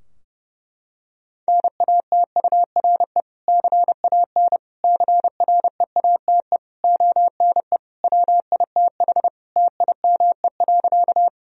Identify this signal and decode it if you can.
Morse code, heard as a keyed tone, about 30 wpm, 700 Hz. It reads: NATURE CAN CREATE ONE WITH TIME.
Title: Sentences from Top 1000 Words - Review Part 3 - 30wpm